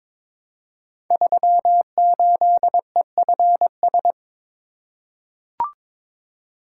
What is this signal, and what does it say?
Morse code, heard as a keyed tone, about 22 wpm, 700 Hz. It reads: 38EFS
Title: Code Groups - 5 Letter:Numbers - 22wpm